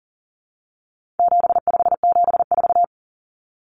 Morse code, heard as a keyed tone, about 40 wpm, 700 Hz. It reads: 7574
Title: Common Cut Numbers - 4-Digit 40wpm